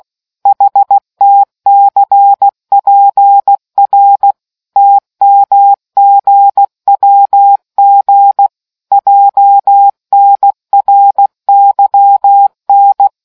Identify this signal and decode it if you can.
Morse code, heard as a keyed tone, about 16 wpm, 800 Hz. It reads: HTCPR TMGWG JNRYN